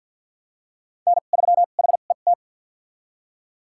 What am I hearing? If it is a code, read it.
Morse code, heard as a keyed tone, about 52 wpm, spaced as for 36 wpm, 700 Hz. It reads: N3HET
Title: USA Call Signs - 5 Characters - 2x Character Spacing - 50wpm